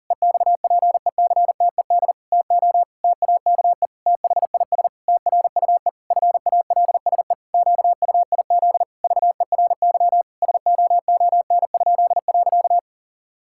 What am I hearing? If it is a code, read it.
Morse code, heard as a keyed tone, about 40 wpm, 700 Hz. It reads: EXPECTED TO TAKE THIS TRUE FALSE QUIZ VERY SOON?.
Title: Pangram Challenge v2 - 40wpm